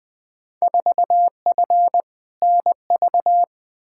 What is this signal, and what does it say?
Morse code, heard as a keyed tone, about 21 wpm, 700 Hz. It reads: 4F NV
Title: Field Day Exchanges Encore - 20wpm